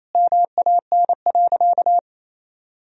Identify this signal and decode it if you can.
Morse code, heard as a keyed tone, about 28 wpm, 700 Hz. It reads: MAN.
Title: Sentences from Top 200 Words and Abbreviations 28wpm